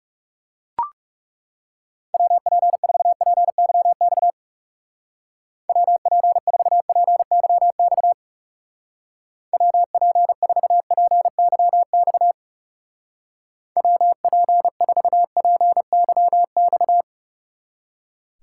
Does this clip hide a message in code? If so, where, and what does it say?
Yes: Morse code from 2.1 to 17.0 s, heard as a keyed tone, about 38 wpm, 700 Hz. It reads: WP4PYX WP4PYX WP4PYX WP4PYX